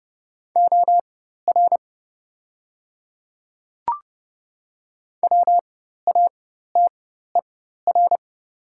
Morse code, heard as a keyed tone, about 30 wpm, spaced as for 14 wpm, 700 Hz. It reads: OR WATER
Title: 4x Character Spacing - Top 100 Words - 30wpm